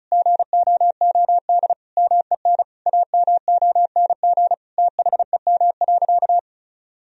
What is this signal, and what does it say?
Morse code, heard as a keyed tone, about 36 wpm, 700 Hz. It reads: GOOD MEN AMONG THEM.
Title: Sentences from Top 1000 Words - Review Part 2 - 35wpm